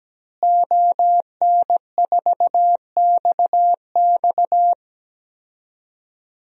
Morse code, heard as a keyed tone, about 17 wpm, 700 Hz. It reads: ON4XX
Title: DXCC-100 Call Signs - 5 Characters - Repeated 2x - 17wpm